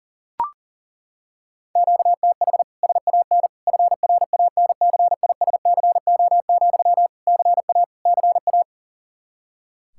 Morse code, heard as a keyed tone, about 40 wpm, 700 Hz. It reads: QTH SAN FRANCISCO, CA CA
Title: K1USN SST QSO Practice 40wpm